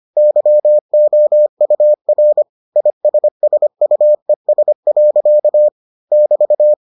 Morse code, heard as a keyed tone, about 25 wpm, 600 Hz. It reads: YOUR ISSUES. <BT>